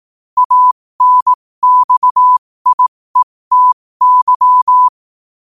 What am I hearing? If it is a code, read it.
Morse code, heard as a keyed tone, about 18 wpm, spaced as for 15 wpm, 1000 Hz. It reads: ANXIETY
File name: qotdfiles / 15wpm